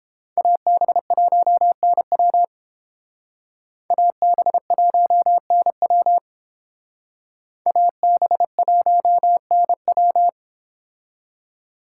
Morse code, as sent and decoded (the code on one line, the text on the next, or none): .- -... .---- -. .-- / .- -... .---- -. .-- / .- -... .---- -. .--
AB1NW AB1NW AB1NW